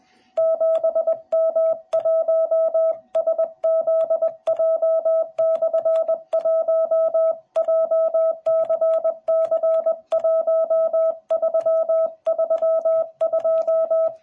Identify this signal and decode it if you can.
Morse code, heard as a keyed tone, about 20 wpm, 650 Hz. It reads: 7M1SZJ/1 JCC1332